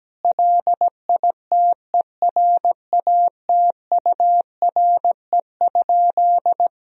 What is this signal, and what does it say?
Morse code, heard as a keyed tone, about 17 wpm, 700 Hz. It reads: LITERATURE?
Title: Sentences from Top 1000 Words - Review Part 3 - 17wpm